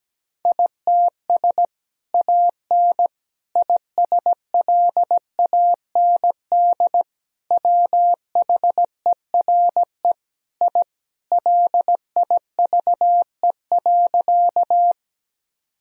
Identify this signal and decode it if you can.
Morse code, heard as a keyed tone, about 17 wpm, 700 Hz. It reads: ITS AN ISLAND WHERE I LIVE.
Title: Sentences from Top 1000 Words - Review Part 3 - 17wpm